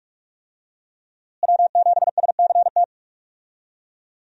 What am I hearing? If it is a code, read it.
Morse code, heard as a keyed tone, about 44 wpm, 700 Hz. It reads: W7SCT
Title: USA Call Signs - 5 Characters - Repeated 2x - 45wpm